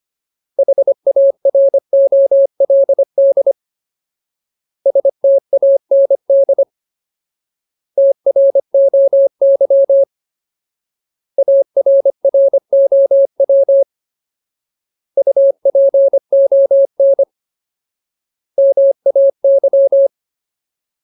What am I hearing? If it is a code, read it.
Morse code, heard as a keyed tone, about 25 wpm, 550 Hz. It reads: HAROLD STAND TROY ARROW UPON MAY
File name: Ultimate Word List / 25WPM Ultimate